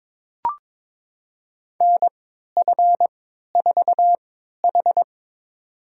Morse code, heard as a keyed tone, about 22 wpm, spaced as for 13 wpm, 700 Hz. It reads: NF4H
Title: USA Call Signs - 4 Characters - 3x Character Spacing - 22wpm